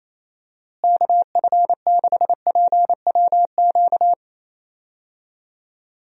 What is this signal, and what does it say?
Morse code, heard as a keyed tone, about 27 wpm, 700 Hz. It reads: KF6PWQ
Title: USA Call Signs - 6 Characters - Repeated 3x - 28wpm